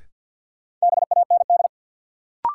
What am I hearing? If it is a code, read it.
Morse code, heard as a keyed tone, about 52 wpm, 700 Hz. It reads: BAND
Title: Sentences from Top 1600 Words - New Word Drill Alt - 50wpm